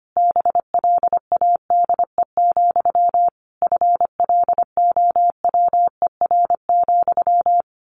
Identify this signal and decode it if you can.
Morse code, heard as a keyed tone, about 25 wpm, 700 Hz. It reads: BLADE, FLOWER,
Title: A Sound Of Thunder 25wpm